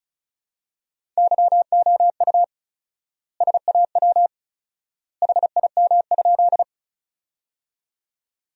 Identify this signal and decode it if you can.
Morse code, heard as a keyed tone, about 36 wpm, 700 Hz. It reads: YOU SAW HIM?
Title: 4x Word-Spacing - Sentences from Top 300 Words - 35wpm